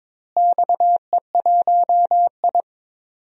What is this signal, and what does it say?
Morse code, heard as a keyed tone, about 22 wpm, 700 Hz. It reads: XE1I
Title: DXCC-100 Call Signs - 4 Characters - Repeated 3x - 22wpm